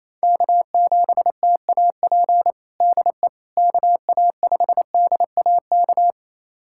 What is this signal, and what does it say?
Morse code, heard as a keyed tone, about 27 wpm, 700 Hz. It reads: K7TAP DE KA5DAK
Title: USA Callsign DE Callsign 28wpm